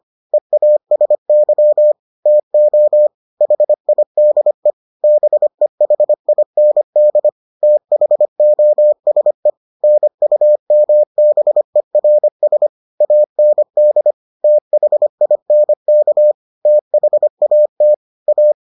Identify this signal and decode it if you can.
Morse code, heard as a keyed tone, about 25 wpm, 600 Hz. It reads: EASY TO HIDE BEHIND THOSE NUMBERS AND THINK THAT A